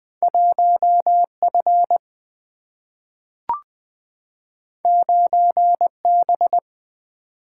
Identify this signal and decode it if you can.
Morse code, heard as a keyed tone, about 20 wpm, 700 Hz. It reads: 1 F 9 B
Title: Single-Letter Single-Number 20wpm